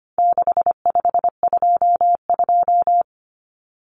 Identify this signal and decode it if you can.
Morse code, heard as a keyed tone, about 25 wpm, 700 Hz. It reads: 6522
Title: Common Cut Numbers - 4-Digit 25wpm